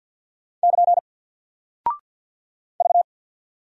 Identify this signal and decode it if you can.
Morse code, heard as a keyed tone, about 48 wpm, 700 Hz. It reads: <KN> V
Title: Single Character (Letter:Number:Punctuation:Prosign) Alt 50wpm